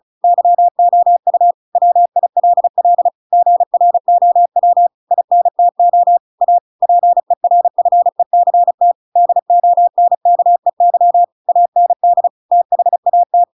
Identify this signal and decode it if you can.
Morse code, heard as a keyed tone, about 34 wpm, 700 Hz. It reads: YOU WILL GROW INTO A PERFECT DONKEY AND THAT